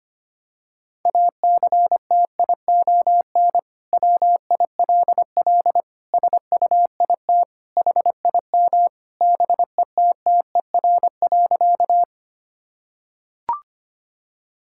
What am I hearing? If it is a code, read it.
Morse code, heard as a keyed tone, about 25 wpm, 700 Hz. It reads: ACTION WILL SUIT HIM BETTER.
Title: Sentences from Top 1600 Words 25wpm